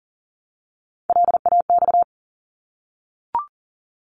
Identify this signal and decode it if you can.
Morse code, heard as a keyed tone, about 40 wpm, 700 Hz. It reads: LAX
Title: ARRL Sections 40wpm